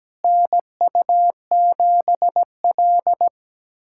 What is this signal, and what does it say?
Morse code, heard as a keyed tone, about 17 wpm, 700 Hz. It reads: NU7L
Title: USA Callsign DE Callsign - 4 Characters - 17wpm